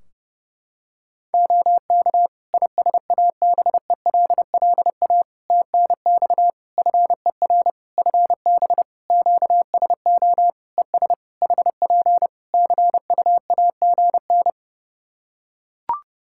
Morse code, heard as a keyed tone, about 28 wpm, 700 Hz. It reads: OK ISABELLA TNX FER FB QSO ES HP CUAGN
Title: Practice QSO - Exchange 4 - Ending - 30wpm